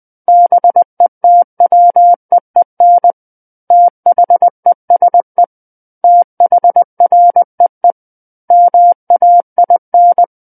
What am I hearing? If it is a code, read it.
Morse code, heard as a keyed tone, about 20 wpm, 700 Hz. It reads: BETWEEN THESE THREE MAIN
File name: 20 WPM